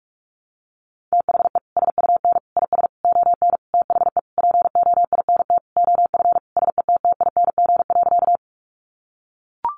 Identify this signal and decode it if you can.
Morse code, heard as a keyed tone, about 44 wpm, 700 Hz. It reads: THE SUN IS ON THE POINT OF SETTING.
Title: Sentences from Top 1300 Words 45wpm